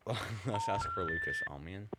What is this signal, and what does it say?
Telephone call progress tone, heard as a telephone network's tone, with speech. Special information tone.